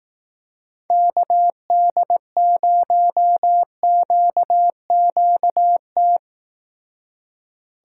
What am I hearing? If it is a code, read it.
Morse code, heard as a keyed tone, about 18 wpm, 700 Hz. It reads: KD0QQT
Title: Speed Racing - USA 2x3 Call Signs 15wpm